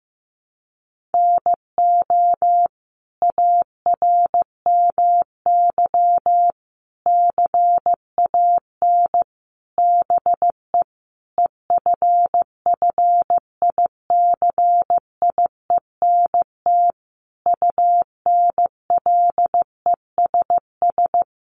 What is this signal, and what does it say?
Morse code, heard as a keyed tone, about 15 wpm, 700 Hz. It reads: NO ARMY CAN BE EFFICIENT UNLESS
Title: Sentences from Top 1300 Words 15wpm